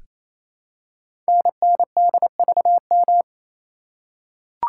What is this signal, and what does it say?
Morse code, heard as a keyed tone, about 29 wpm, 700 Hz. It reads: NNDVM
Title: Code Groups - 5 Letters - 28wpm